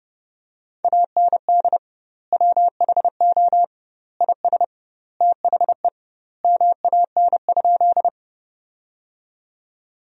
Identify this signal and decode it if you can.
Morse code, heard as a keyed tone, about 30 wpm, 700 Hz. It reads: AND WHO IS THE MAN?
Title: 2x Word-Spacing - Sentences from Top 200 Words - 30wpm